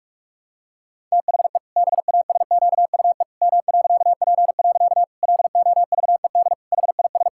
Morse code, heard as a keyed tone, about 46 wpm, 700 Hz. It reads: THE BASQUE M.P. LOVED HIS
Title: Pangram Challenge v2 - 45wpm